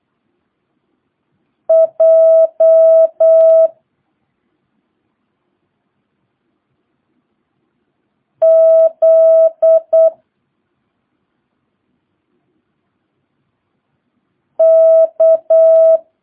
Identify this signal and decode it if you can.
Morse code, heard as a keyed tone, about 8 wpm, 650 Hz. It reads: J Z K
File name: jh8cbh / 2022cw